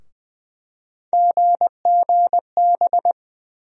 Morse code, heard as a keyed tone, about 20 wpm, 700 Hz. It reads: GGB
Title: USA Call Sign Postfixes - 3 Letters 20wpm